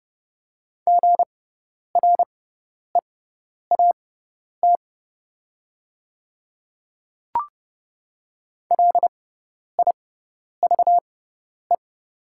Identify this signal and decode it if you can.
Morse code, heard as a keyed tone, about 30 wpm, spaced as for 10 wpm, 690 Hz. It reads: GREAT LIVE
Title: 6x Character Spacing - Top 300 Words - 30wpm